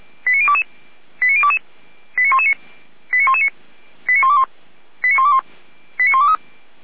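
Five-tone selective call (ZVEI), heard as quick runs of five tones, tones of about 70 ms. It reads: 89022 89022 89119 89119 89121 89121 89123
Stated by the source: ZVEI five-tone selective call received on an SDR.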